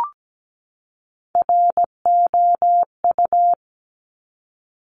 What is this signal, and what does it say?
Morse code, heard as a keyed tone, about 17 wpm, 700 Hz. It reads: ROU
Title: Top Three Letter Combinations in Words 17wpm